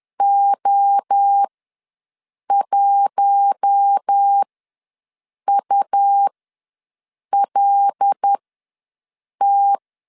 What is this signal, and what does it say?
Morse code, heard as a keyed tone, about 10 wpm, spaced as for 6 wpm, 800 Hz. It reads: O1ULT
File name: LEZIONI / 7 GF9Z / 7 G ws 10wpm 3843